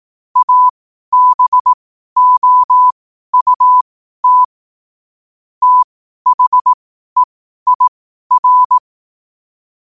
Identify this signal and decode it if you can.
Morse code, heard as a keyed tone, about 18 wpm, spaced as for 12 wpm, 1000 Hz. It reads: ABOUT THEIR
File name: qotdfiles / 12wpm